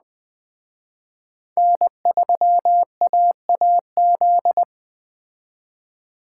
Morse code, heard as a keyed tone, about 21 wpm, 700 Hz. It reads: N3AAZ